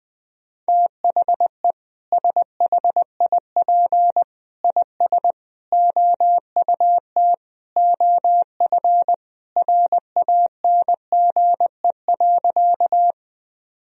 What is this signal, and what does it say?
Morse code, heard as a keyed tone, about 20 wpm, 700 Hz. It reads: THE SHIP IS OUT OF RANGE.
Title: Sentences from Top 1000 Words - Review Part 4 - 20wpm